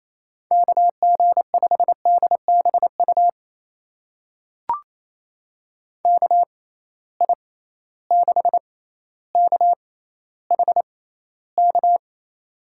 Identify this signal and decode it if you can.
Morse code, heard as a keyed tone, about 28 wpm, spaced as for 14 wpm, 700 Hz. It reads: KG5DBU KI6KHK
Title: USA Call Signs - 6 Characters - 6x Character Spacing - 28wpm